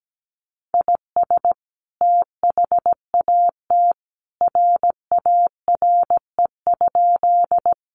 Morse code, heard as a keyed tone, about 17 wpm, 700 Hz. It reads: IS THAT RARE?
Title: Sentences from Top 1100 Words 17wpm